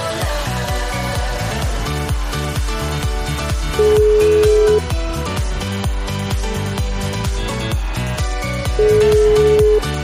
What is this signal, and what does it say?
Ringback tone (Europe, 425 Hz), with noise.